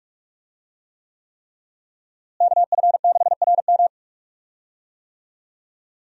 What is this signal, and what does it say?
Morse code, heard as a keyed tone, about 41 wpm, 700 Hz. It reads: KF6RM
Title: USA Call Signs - 5 Characters - Repeated 3x - 45wpm